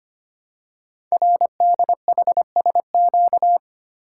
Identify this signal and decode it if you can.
Morse code, heard as a keyed tone, about 25 wpm, 700 Hz. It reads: RDHSQ
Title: Code Groups - 5 Letters - 25wpm